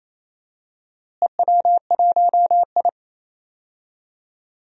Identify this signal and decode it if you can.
Morse code, heard as a keyed tone, about 28 wpm, 700 Hz. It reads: EW1I
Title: DXCC-100 Call Signs - 4 Characters - Repeated 2x - 28wpm